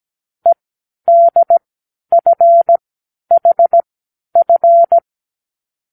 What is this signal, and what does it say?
Morse code, heard as a keyed tone, about 17 wpm, spaced as for 10 wpm, 680 Hz. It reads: EDFHF
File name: k7qo / Lesson18(A-L)